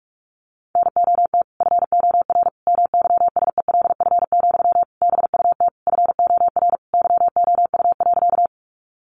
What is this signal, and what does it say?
Morse code, heard as a keyed tone, about 46 wpm, 700 Hz. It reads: NOT FOR MYSELF, BUT FOR YOU.
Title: Sentences from Top 1300 Words 45wpm